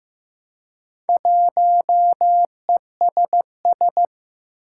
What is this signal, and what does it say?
Morse code, heard as a keyed tone, about 15 wpm, 700 Hz. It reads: 1ESS
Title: Four Challenging Letter-Number Combinations 15wpm